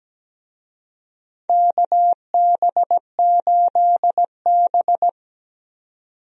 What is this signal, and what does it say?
Morse code, heard as a keyed tone, about 17 wpm, 700 Hz. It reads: KB8B
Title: USA Call Signs - 4 Characters - 17wpm